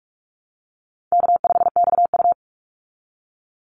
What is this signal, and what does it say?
Morse code, heard as a keyed tone, about 44 wpm, 700 Hz. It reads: K5XU
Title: Call Signs Repeated 3x - 45wpm